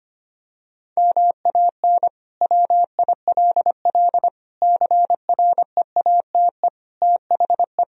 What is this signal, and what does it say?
Morse code, heard as a keyed tone, about 25 wpm, 700 Hz. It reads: MAN WILL CREATE THE
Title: Sentences from Top 1000 Words - Review Part 4 - 25wpm